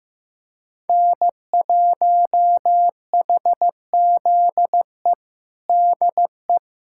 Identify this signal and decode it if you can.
Morse code, heard as a keyed tone, about 15 wpm, 700 Hz. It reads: N1HZE DE
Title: USA Callsign DE Callsign 15wpm